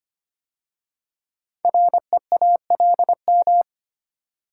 Morse code, heard as a keyed tone, about 25 wpm, 700 Hz. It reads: REALM
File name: 5 Letter Words - First 3 Letters Given - 25wpm